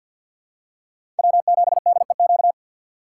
Morse code, heard as a keyed tone, about 50 wpm, 700 Hz. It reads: W7DEQ
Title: USA Call Signs - 5 Characters - 8x Character Spacing - 50wpm